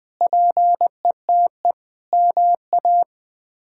Morse code, heard as a keyed tone, about 20 wpm, 700 Hz. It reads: PETE MA
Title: K1USN SST QSO Practice - 20wpm